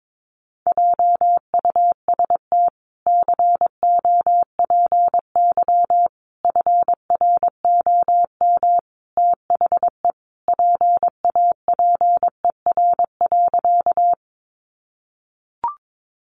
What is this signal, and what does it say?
Morse code, heard as a keyed tone, about 22 wpm, 700 Hz. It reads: JUST COPY FROM THE PAPER.
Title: Sentences from Top 1400 Words 22wpm